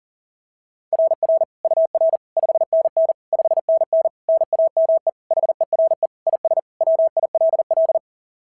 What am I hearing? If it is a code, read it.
Morse code, heard as a keyed tone, about 40 wpm, 690 Hz. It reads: RR UR 5NN 5NN NAME HERE IS WILL
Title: SOTA QSO Practice 40wpm